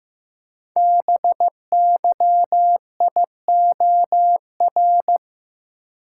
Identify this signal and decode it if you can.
Morse code, heard as a keyed tone, about 15 wpm, 700 Hz. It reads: BYIOR
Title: Code Groups - 5 Letters - 15wpm